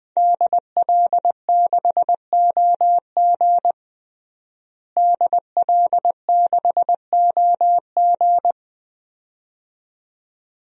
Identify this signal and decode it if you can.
Morse code, heard as a keyed tone, about 20 wpm, 700 Hz. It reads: DL6OG DL6OG
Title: DXCC-100 Call Signs - 5 Characters - Repeated 2x - 20wpm